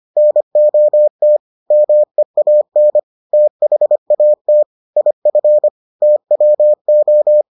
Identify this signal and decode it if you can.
Morse code, heard as a keyed tone, about 25 wpm, 600 Hz. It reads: NOT MEAN THAT IF TWO